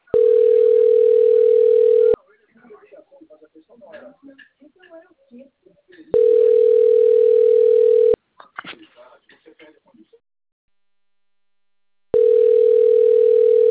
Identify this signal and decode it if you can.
Ringback tone (North America), 440 + 480 Hz, with speech.
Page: Itatiba